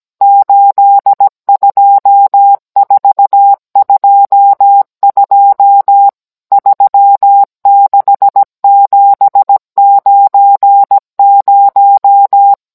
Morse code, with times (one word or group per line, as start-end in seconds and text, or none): 0.2-6.1 82422
6.5-12.5 36790